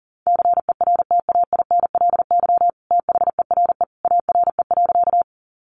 Morse code, heard as a keyed tone, about 40 wpm, 700 Hz. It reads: CERTAINLY THERE ARE.
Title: Sentences from Top 900 Words 40wpm